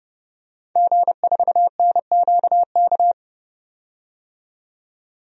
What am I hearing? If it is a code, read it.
Morse code, heard as a keyed tone, about 30 wpm, 700 Hz. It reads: G4NQK
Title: DXCC-100 Call Signs - 5 Characters - 30wpm